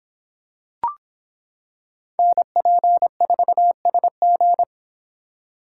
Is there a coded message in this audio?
Yes: Morse code from 2.2 to 4.6 s, heard as a keyed tone, about 26 wpm, 700 Hz. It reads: NP4SG